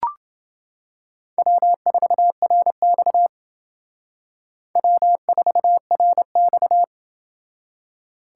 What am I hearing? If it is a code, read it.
Morse code, heard as a keyed tone, about 28 wpm, 700 Hz. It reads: W4RX W4RX